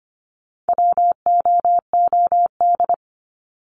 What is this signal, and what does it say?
Morse code, heard as a keyed tone, about 25 wpm, 700 Hz. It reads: WOOD
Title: Top 300 Words 25wpm